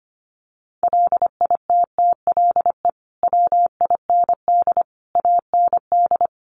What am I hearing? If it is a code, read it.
Morse code, heard as a keyed tone, about 25 wpm, 700 Hz. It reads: LITTLE WIND AND
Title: Sentences from Top 1300 Words 25wpm